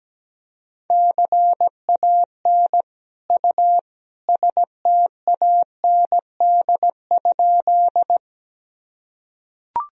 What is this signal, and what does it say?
Morse code, heard as a keyed tone, about 17 wpm, 700 Hz. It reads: CAN U STAND?
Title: Sentences from Top 300 Words and Abbreviations 17wpm